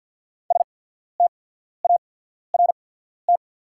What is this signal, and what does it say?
Morse code, heard as a keyed tone, about 48 wpm, spaced as for 14 wpm, 690 Hz. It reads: START
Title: 8x Character Spacing - Top 300 Words - 50wpm